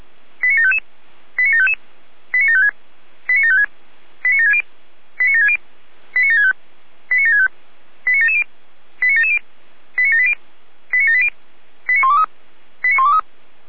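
Five-tone selective call (ZVEI), heard as quick runs of five tones, tones of about 80 ms. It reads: 89755 89755 89756 89756 89760 89760 89765 89765 89770 89770 89780 89780 89123 89123
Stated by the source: ZVEI five-tone selective call received on an SDR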